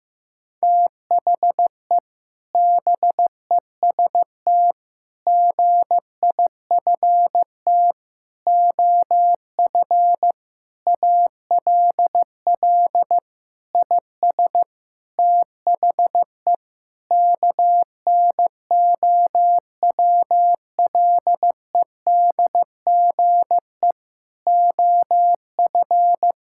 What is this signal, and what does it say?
Morse code, heard as a keyed tone, about 16 wpm, 700 Hz. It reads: THE BEST GIFT OF ALL IS THE KNOWLEDGE OF
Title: Sentences from Top 1000 Words - Review Part 6 - 15wpm